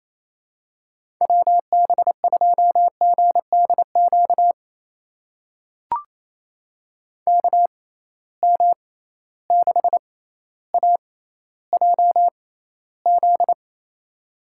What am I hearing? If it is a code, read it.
Morse code, heard as a keyed tone, about 28 wpm, spaced as for 14 wpm, 700 Hz. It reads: WB2GDQ KM6AJZ
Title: USA Call Signs - 6 Characters - 6x Character Spacing - 28wpm